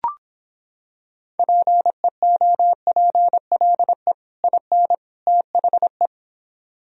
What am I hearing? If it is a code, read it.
Morse code, heard as a keyed tone, about 26 wpm, 700 Hz. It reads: PEOPLE IN THE